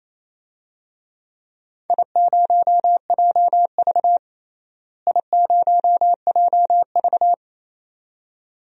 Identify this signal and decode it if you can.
Morse code, heard as a keyed tone, about 28 wpm, 700 Hz. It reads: I0JV I0JV
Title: DXCC-100 Call Signs - 4 Characters - Repeated 2x - 28wpm